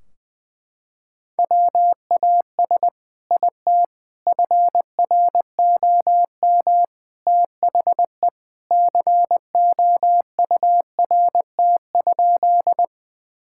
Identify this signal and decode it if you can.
Morse code, heard as a keyed tone, about 21 wpm, 700 Hz. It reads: WAS IT FROM THE COURT?
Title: Sentences from Top 1600 Words 20wpm